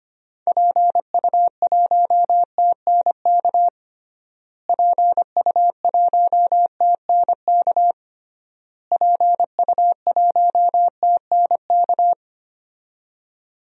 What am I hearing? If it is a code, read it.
Morse code, heard as a keyed tone, about 25 wpm, 700 Hz. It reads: PU1TNK PU1TNK PU1TNK
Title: DXCC-100 Call Signs - 6 Characters - Repeated 3x - 25wpm